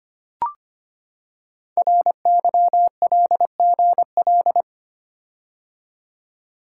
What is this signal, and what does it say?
Morse code, heard as a keyed tone, about 25 wpm, 700 Hz. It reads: RYLGL
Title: Code Groups - 5 Letters - 25wpm